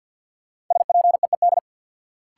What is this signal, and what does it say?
Morse code, heard as a keyed tone, about 52 wpm, 700 Hz. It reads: SPEED
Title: Sentences from Top 800 Words - New Word Drill - 50wpm